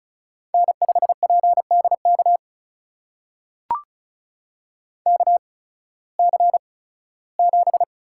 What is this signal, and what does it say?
Morse code, heard as a keyed tone, about 36 wpm, spaced as for 19 wpm, 700 Hz. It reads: N5PDK KC7
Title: USA Call Signs - 5 Characters - 8x Character Spacing - 35wpm